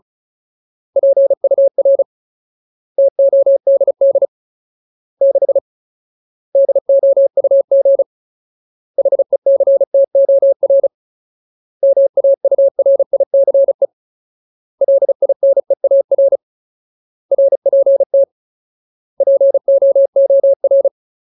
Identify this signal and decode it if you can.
Morse code, heard as a keyed tone, about 34 wpm, 550 Hz. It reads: PUR TODD 6 DOUG HECTOR MAURICE LINEAR RPT POOR
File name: Ultimate Word List / 35WPM Ultimate